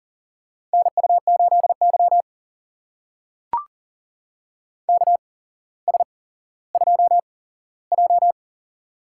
Morse code, heard as a keyed tone, about 43 wpm, spaced as for 17 wpm, 700 Hz. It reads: NU8Y KS2J
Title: USA Call Signs - 4 Characters - 8x Character Spacing - 40wpm